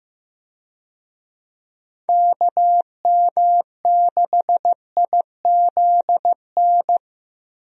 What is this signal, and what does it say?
Morse code, heard as a keyed tone, about 15 wpm, 700 Hz. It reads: KM6IZN
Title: USA Call Signs - 6 Characters - Repeated 3x - 15wpm